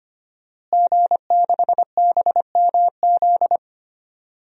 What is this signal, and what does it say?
Morse code, heard as a keyed tone, about 25 wpm, 700 Hz. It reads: G6BMZ
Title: DXCC-100 Call Signs - 5 Characters - Repeated 3x - 25wpm